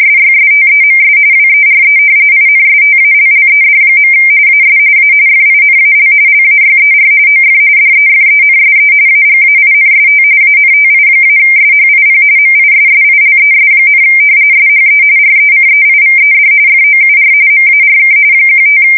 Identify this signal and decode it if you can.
RTTY, about 45.45 baud. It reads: SYSTEMS WERE INSTALLED. AS A FINAL LAYER OF DEFENSE AGAINST ANTI-SHIP MISSILES, SHE RECEIVED FOUR MARK 15
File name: fsk170 / battleshipnj